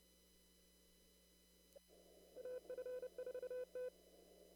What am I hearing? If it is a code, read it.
Morse code, heard as a keyed tone, about 30 wpm, 500 Hz. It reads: AF4T
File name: CW / kt5j wpx cw-op-k5pi